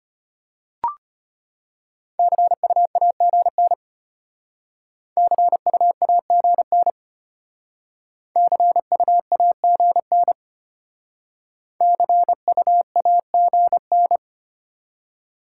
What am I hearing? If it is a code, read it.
Morse code, heard as a keyed tone, about 34 wpm, 700 Hz. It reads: CU AGN CU AGN CU AGN CU AGN